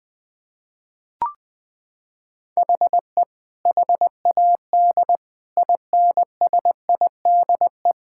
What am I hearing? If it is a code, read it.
Morse code, heard as a keyed tone, about 20 wpm, 700 Hz. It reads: HE HAD INSIDE
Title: Sentences from Top 1000 Words - Review Part 6 - 20wpm